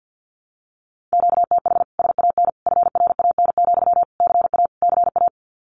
Morse code, heard as a keyed tone, about 48 wpm, 700 Hz. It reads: QTH SAN FRAN, CA CA
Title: Common QSO Phrases Encore 50wpm